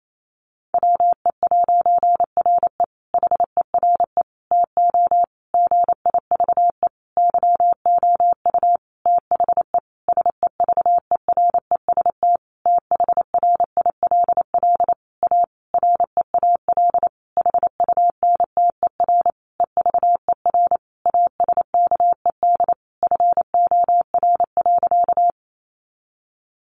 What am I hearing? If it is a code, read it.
Morse code, heard as a keyed tone, about 28 wpm, 700 Hz. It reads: WE'RE HERE TO GIVE YOU THE SEVEREST THRILL A REAL HUNTER EVER ASKED FOR.
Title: A Sound Of Thunder 28wpm